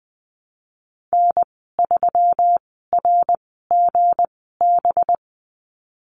Morse code, heard as a keyed tone, about 20 wpm, spaced as for 15 wpm, 700 Hz. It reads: N3RGB
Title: 219 Lesson 35g - Number 0 - Call Signs 20wpm